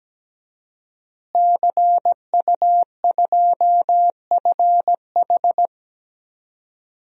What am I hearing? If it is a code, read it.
Morse code, heard as a keyed tone, about 17 wpm, 700 Hz. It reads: CU2FH
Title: DXCC-100 Call Signs - 5 Characters - Repeated 3x - 17wpm